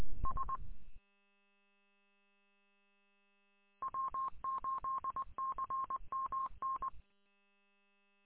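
Morse code, heard as a keyed tone, about 20 wpm, 1070 Hz. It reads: S W8CMN
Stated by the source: Morse code in W8CMN 146780 combined audio archive recording.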